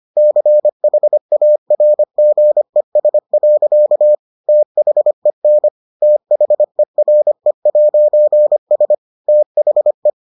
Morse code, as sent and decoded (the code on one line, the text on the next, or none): -.-. .... .- .-. --. ... .-.-.- / - .... -. / - .... .-. .----. ... / - .... .
CHARGES. THEN THERE'S THE